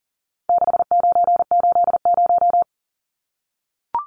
Morse code, heard as a keyed tone, about 40 wpm, 700 Hz. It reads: 6980
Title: Numbers - 4-Digit 40wpm